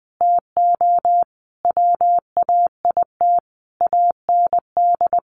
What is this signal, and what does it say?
Morse code, heard as a keyed tone, about 20 wpm, 700 Hz. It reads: TO WAIT AND
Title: Sentences from Top 1000 Words 20wpm